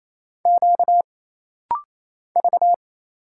Morse code, heard as a keyed tone, about 27 wpm, 700 Hz. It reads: Q V
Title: Single Letters - Rapid-Fire 28wpm